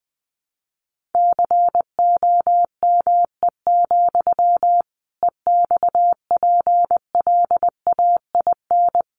Morse code, heard as a keyed tone, about 20 wpm, 700 Hz. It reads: COME, EXPLAIN
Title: Sentences from Top 1100 Words 20wpm